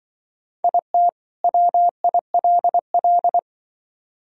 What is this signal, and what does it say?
Morse code, heard as a keyed tone, about 24 wpm, 700 Hz. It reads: IT WILL
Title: Speed-Racing - Sets of 2 Words 20wpm